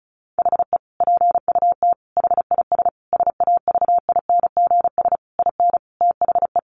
Morse code, heard as a keyed tone, about 34 wpm, 700 Hz. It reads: HE PUT HIS SAVINGS IN THE
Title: Sentences from Top 1200 Words 35wpm